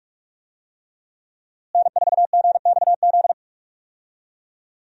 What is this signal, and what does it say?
Morse code, heard as a keyed tone, about 47 wpm, 700 Hz. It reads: N4GXZ
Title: USA Call Signs - 5 Characters - Repeated 3x - 45wpm